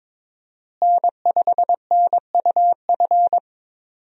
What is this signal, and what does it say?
Morse code, heard as a keyed tone, about 22 wpm, 700 Hz. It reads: N5NUF
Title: USA Call Signs - 5 Characters - Repeated 3x - 22wpm